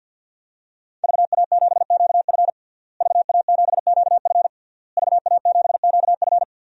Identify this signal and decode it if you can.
Morse code, heard as a keyed tone, about 48 wpm, 700 Hz. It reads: VA7QF VA7QF VA7QF
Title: DXCC-100 Call Signs - 5 Characters - Repeated 3x - 50wpm